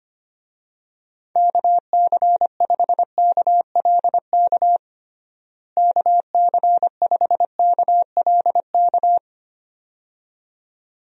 Morse code, heard as a keyed tone, about 25 wpm, 700 Hz. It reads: KC5KLK KC5KLK
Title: USA Call Signs - 6 Characters - Repeated 2x - 25wpm